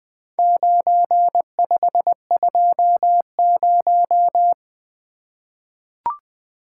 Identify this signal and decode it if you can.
Morse code, heard as a keyed tone, about 20 wpm, 700 Hz. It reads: 9520
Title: Four Letter-Number Combinations - 20wpm